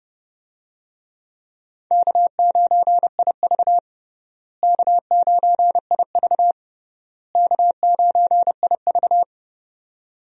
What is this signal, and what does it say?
Morse code, heard as a keyed tone, about 32 wpm, 700 Hz. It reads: K9IV K9IV K9IV
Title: USA Call Signs - 4 Characters - Repeated 3x - 30wpm